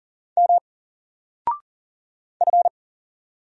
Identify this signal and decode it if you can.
Morse code, heard as a keyed tone, about 40 wpm, 700 Hz. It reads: M F
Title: Single Letter-Number 40wpm